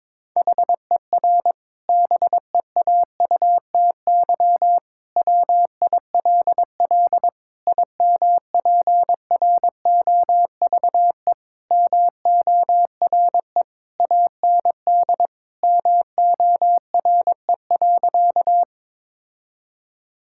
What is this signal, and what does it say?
Morse code, heard as a keyed tone, about 22 wpm, 700 Hz. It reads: HER BEAUTY WILL IMPROVE MORE AND MORE.
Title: Sentences from Top 1000 Words - Review Part 5 - 22wpm